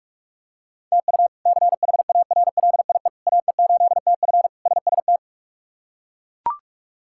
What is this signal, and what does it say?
Morse code, heard as a keyed tone, about 41 wpm, 700 Hz. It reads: TU CHARLIE AE8TF SST
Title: K1USN SST QSO Practice - 45wpm